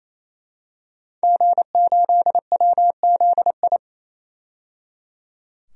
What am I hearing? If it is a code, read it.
Morse code, heard as a keyed tone, about 28 wpm, 700 Hz. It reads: G8WZI
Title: DXCC-100 Call Signs - 5 Characters - 28wpm